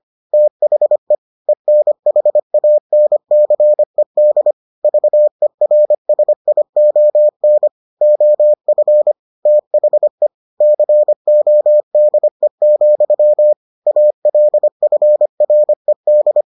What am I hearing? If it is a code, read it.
Morse code, heard as a keyed tone, about 25 wpm, 600 Hz. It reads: THE ENHANCED VERSION OF THE CODE, ALFRED